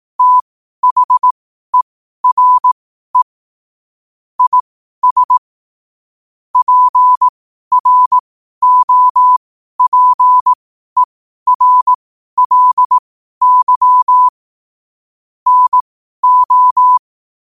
Morse code, heard as a keyed tone, about 18 wpm, spaced as for 12 wpm, 1000 Hz. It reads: THERE IS PROPERLY NO
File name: qotdfiles / 12wpm